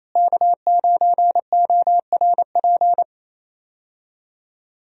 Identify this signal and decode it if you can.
Morse code, heard as a keyed tone, about 28 wpm, 700 Hz. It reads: K9ORP
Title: USA Call Signs - 5 Characters - Repeated 2x - 28wpm